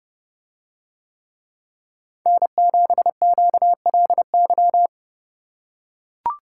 Morse code, heard as a keyed tone, about 29 wpm, 700 Hz. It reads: N7QLY